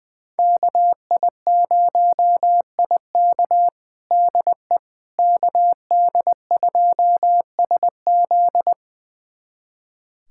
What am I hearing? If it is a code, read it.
Morse code, heard as a keyed tone, about 21 wpm, 700 Hz. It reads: KI0IK DE KD2SZ
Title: USA Callsign DE Callsign - 5 Characters - 20wpm